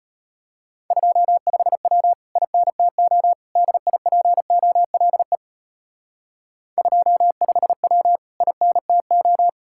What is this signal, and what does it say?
Morse code, heard as a keyed tone, about 31 wpm, 700 Hz. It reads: 25W INTO DIPOLE 25W INTO
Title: Speed-Racing Common QSO Phrases Encore - 25wpm